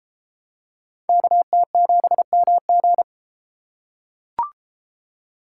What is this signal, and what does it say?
Morse code, heard as a keyed tone, about 30 wpm, 700 Hz. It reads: KT7MG